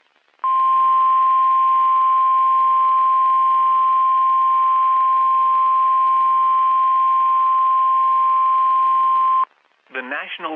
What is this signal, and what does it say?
Emergency Alert System (SAME) tones (the 1050 Hz attention tone), with speech. It is heard from 0.4 to 9.5 s.
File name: eas messages / EAS WXM31 Elmira